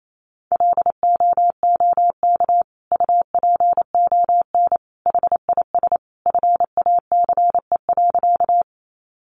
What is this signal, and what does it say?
Morse code, heard as a keyed tone, about 28 wpm, 700 Hz. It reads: LOOK UPON HIS FACE.
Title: Sentences from Top 1100 Words 28wpm